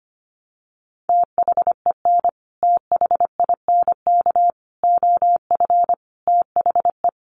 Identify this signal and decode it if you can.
Morse code, heard as a keyed tone, about 25 wpm, 700 Hz. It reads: THEN THINK OF THE
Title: Sentences from Top 1000 Words 25wpm